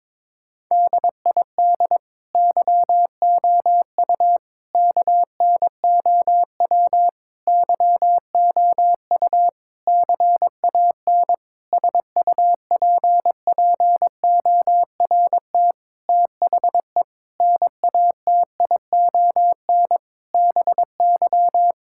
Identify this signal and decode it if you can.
Morse code, heard as a keyed tone, about 22 wpm, 700 Hz. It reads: DID YOU KNOW YOU CAN SUPPORT THE NATION BY
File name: Sets of 9 Words - 22wpm